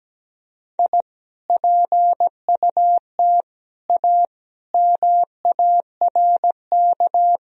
Morse code, heard as a keyed tone, about 17 wpm, 700 Hz. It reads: I PUT A MARK
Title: Sentences from Top 1000 Words - Review Part 1 - 17wpm